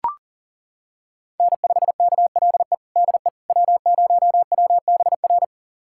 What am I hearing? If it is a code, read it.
Morse code, heard as a keyed tone, about 40 wpm, 700 Hz. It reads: N5KLE DE W0WBR
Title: USA Callsign DE Callsign - 5 Characters - 40wpm